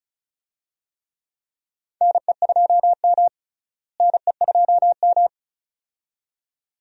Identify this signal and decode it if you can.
Morse code, heard as a keyed tone, about 36 wpm, 700 Hz. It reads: NE2M NE2M
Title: USA Call Signs - 4 Characters - Repeated 2x - 35wpm